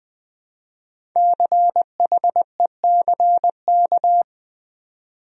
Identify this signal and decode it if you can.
Morse code, heard as a keyed tone, about 20 wpm, 700 Hz. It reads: CHECK
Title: Speed-Racing Top 500 Words 17wpm